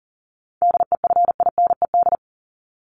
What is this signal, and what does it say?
Morse code, heard as a keyed tone, about 40 wpm, 700 Hz. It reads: DEFINED
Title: ED Suffix - Top 100 Words with Base Words 40wpm